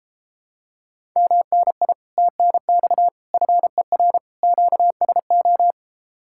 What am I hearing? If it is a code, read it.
Morse code, heard as a keyed tone, about 33 wpm, 700 Hz. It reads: MNI TNX FER QSO